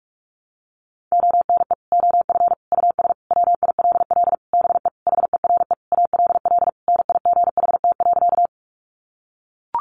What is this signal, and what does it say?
Morse code, heard as a keyed tone, about 45 wpm, 700 Hz. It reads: ONE OF US WILL BE HERE ALL NIGHT.
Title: Sentences from Top 300 Words 45wpm